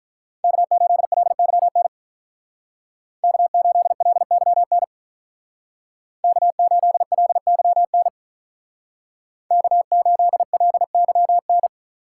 Morse code, heard as a keyed tone, about 43 wpm, 700 Hz. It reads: K8LYN K8LYN K8LYN K8LYN
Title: Speed Racing - USA 1x3 Call Signs 35wpm